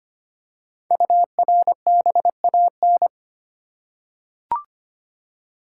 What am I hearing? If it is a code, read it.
Morse code, heard as a keyed tone, about 25 wpm, 700 Hz. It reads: URBAN
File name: 5 Letter Words - First 3 Letters Given - 25wpm